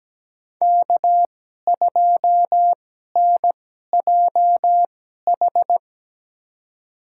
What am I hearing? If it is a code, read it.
Morse code, heard as a keyed tone, about 17 wpm, spaced as for 12 wpm, 700 Hz. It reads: K2NJH
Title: USA Call Signs - 2x Character Spacing - 17wpm